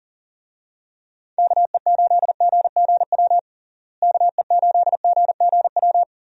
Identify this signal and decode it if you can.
Morse code, heard as a keyed tone, about 40 wpm, 700 Hz. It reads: KE8GGW KE8GGW
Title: USA Call Signs - 6 Characters - Repeated 3x - 40wpm